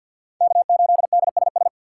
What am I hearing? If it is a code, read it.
Morse code, heard as a keyed tone, about 46 wpm, 690 Hz. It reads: K8DSS
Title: K1USN SST QSO Practice - 50wpm